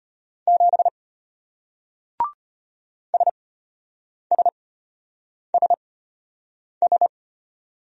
Morse code, heard as a keyed tone, about 33 wpm, spaced as for 8 wpm, 700 Hz. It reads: 7 SSSS